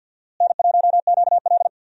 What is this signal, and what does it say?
Morse code, heard as a keyed tone, about 52 wpm, 700 Hz. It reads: N1QL
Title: USA Call Signs - 4 Characters - Repeated 2x - 50wpm